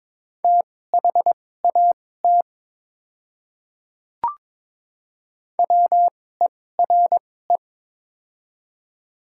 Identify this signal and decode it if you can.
Morse code, heard as a keyed tone, about 22 wpm, spaced as for 16 wpm, 700 Hz. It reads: THAT WERE
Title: 2x Character Spacing - Top 100 Words - 22wpm